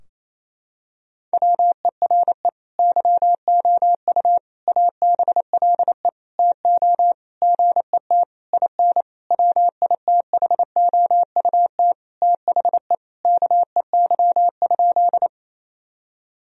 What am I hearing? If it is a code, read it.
Morse code, heard as a keyed tone, about 28 wpm, 700 Hz. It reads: WERE YOU ABLE TO GET IN WITHOUT THE KEY?
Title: Sentences from Top 1000 Words - Review Part 4 - 28wpm